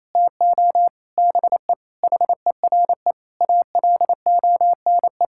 Morse code, heard as a keyed tone, about 28 wpm, 700 Hz. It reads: TO BE HERE ALONE
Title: Sentences from Top 1000 Words - Review Part 5 - 28wpm